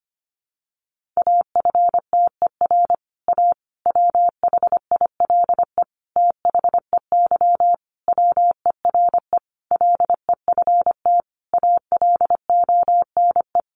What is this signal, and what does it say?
Morse code, heard as a keyed tone, about 25 wpm, 700 Hz. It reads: AFTER A WHILE THEY WERE LEFT ALONE
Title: Sentences from Top 1000 Words 25wpm